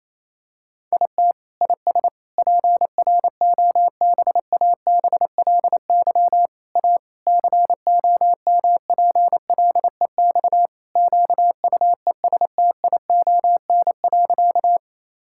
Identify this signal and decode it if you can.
Morse code, heard as a keyed tone, about 28 wpm, 700 Hz. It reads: IT IS PROBABLY A COMPLEX QUESTION.
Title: Sentences from Top 1000 Words - Review Part 6 - 28wpm